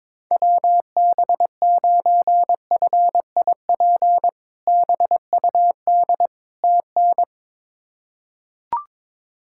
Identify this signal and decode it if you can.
Morse code, heard as a keyed tone, about 22 wpm, 700 Hz. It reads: WB9FIP BUD TN
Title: K1USN SST QSO Practice - 22wpm